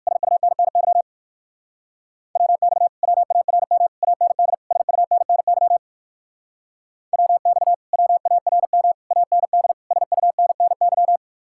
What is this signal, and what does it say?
Morse code, heard as a keyed tone, about 56 wpm, 700 Hz. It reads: SUNNY WX WARM AND SUNNY WX WARM AND SUNNY